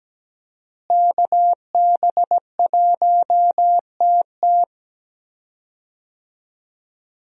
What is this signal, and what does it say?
Morse code, heard as a keyed tone, about 17 wpm, 700 Hz. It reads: KB1TT
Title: USA Call Signs - 5 Characters - Repeated 2x - 17wpm